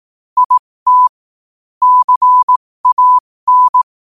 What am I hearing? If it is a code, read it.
Morse code, heard as a keyed tone, about 18 wpm, spaced as for 15 wpm, 1000 Hz. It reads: IT CAN
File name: qotdfiles / 15wpm